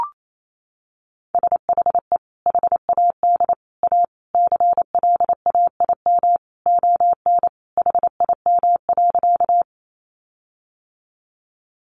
Morse code, heard as a keyed tone, about 28 wpm, 700 Hz. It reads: SHE HAD A CLAIM ON HIM.
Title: Sentences from Top 1500 Words 28wpm